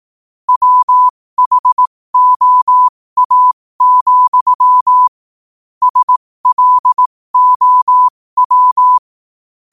Morse code, heard as a keyed tone, about 18 wpm, spaced as for 15 wpm, 1000 Hz. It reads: WHOA, SLOW
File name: qotdfiles / 15wpm